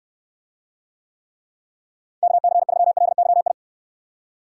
Morse code, heard as a keyed tone, about 62 wpm, 700 Hz. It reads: KC3LYI